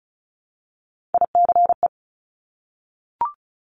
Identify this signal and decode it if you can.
Morse code, heard as a keyed tone, about 36 wpm, 700 Hz. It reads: ICE
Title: Multi-letter Phonemes Repeated 3x 35wpm